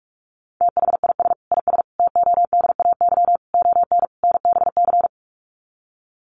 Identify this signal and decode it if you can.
Morse code, heard as a keyed tone, about 44 wpm, 700 Hz. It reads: THIS IS TODAY ON NBC
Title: Sets of 5 Words Encore 45wpm